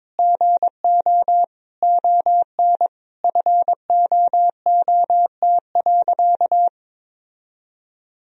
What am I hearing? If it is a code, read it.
Morse code, heard as a keyed tone, about 22 wpm, 700 Hz. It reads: GO ON FOOT.
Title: Sentences from Top 1000 Words - Review Part 3 - 22wpm